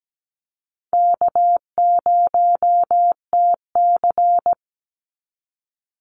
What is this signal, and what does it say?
Morse code, heard as a keyed tone, about 17 wpm, 700 Hz. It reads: K0TC
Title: Call Signs Repeated 2x - 17wpm